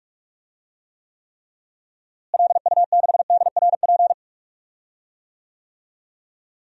Morse code, heard as a keyed tone, about 46 wpm, 700 Hz. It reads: LU6DRP
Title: DXCC-100 Call Signs - 6 Characters - Repeated 3x - 45wpm